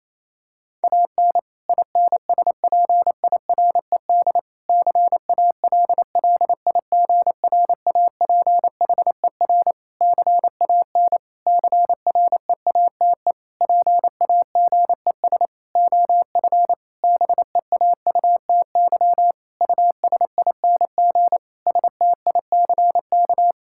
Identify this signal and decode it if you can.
Morse code, heard as a keyed tone, about 27 wpm, 700 Hz. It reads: AN INSPIRED CALLIGRAPHER CAN CREATE PAGES OF BEAUTY USING STICK
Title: Pangram Challenge v2 - 28wpm